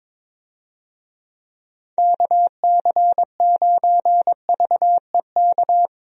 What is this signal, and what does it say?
Morse code, heard as a keyed tone, about 21 wpm, 700 Hz. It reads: KC9VEK
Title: USA Call Signs - 6 Characters - Repeated 3x - 22wpm